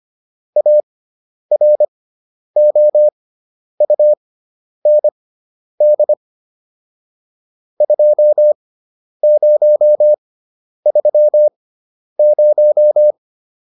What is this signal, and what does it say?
Morse code, heard as a keyed tone, about 25 wpm, spaced as for 10 wpm, 600 Hz. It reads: AROUND 2030